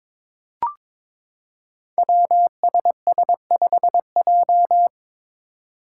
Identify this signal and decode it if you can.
Morse code, heard as a keyed tone, about 22 wpm, 700 Hz. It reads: WSS5J
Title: Five Challenging Letter-Number Combinations 22wpm